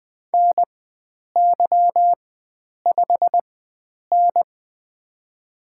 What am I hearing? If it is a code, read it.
Morse code, heard as a keyed tone, about 21 wpm, spaced as for 9 wpm, 700 Hz. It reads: NY5N
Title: USA Call Signs - 4 Characters - 4x Character Spacing - 20wpm